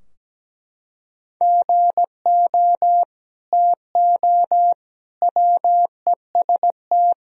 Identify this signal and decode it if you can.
Morse code, heard as a keyed tone, about 17 wpm, 700 Hz. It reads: GO TO WEST
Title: Sentences from Top 1000 Words - Review Part 2 - 17wpm